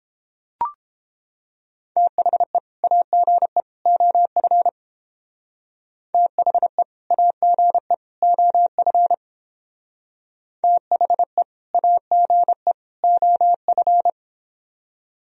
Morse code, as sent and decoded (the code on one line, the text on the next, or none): - .... / .- --. / --- ..-. / - .... / .- --. / --- ..-. / - .... / .- --. / --- ..-.
THE AGE OF THE AGE OF THE AGE OF